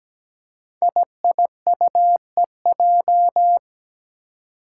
Morse code, heard as a keyed tone, about 17 wpm, 700 Hz. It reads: IIUEJ
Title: Five Challenging Letter-Number Combinations 17wpm